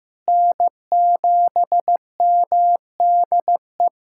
Morse code, heard as a keyed tone, about 15 wpm, 700 Hz. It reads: N7MDE
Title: USA Callsign DE Callsign - 5 Characters - 15wpm